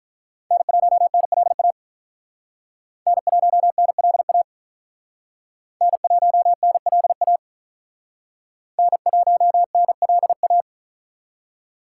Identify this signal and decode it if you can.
Morse code, heard as a keyed tone, about 45 wpm, 700 Hz. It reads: N1NLA N1NLA N1NLA N1NLA